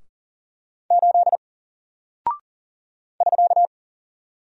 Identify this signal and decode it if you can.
Morse code, heard as a keyed tone, about 40 wpm, 700 Hz. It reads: 8 <SK>
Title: Single Character (Letter:Number:Punctuation:Prosign) Alt 40wpm